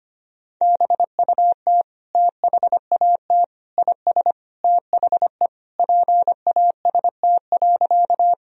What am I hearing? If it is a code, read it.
Morse code, heard as a keyed tone, about 25 wpm, 700 Hz. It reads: BUT THAT IS THE PAST.
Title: Sentences from Top 1000 Words - Review Part 3 - 25wpm